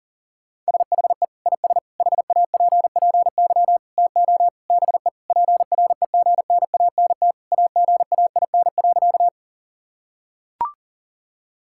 Morse code, heard as a keyed tone, about 37 wpm, 700 Hz. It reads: SHE IS HAPPY TO BE PREGNANT AGAIN.
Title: Sentences from Top 1000 Words - Review Part 6 - 40wpm